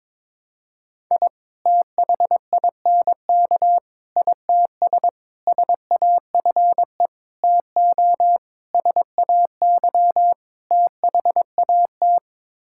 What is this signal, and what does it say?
Morse code, heard as a keyed tone, about 22 wpm, 700 Hz. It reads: I THINK ITS SAFE TO SAY THAT
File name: practice / 22wpm / Sets of 7 Words - 22wpm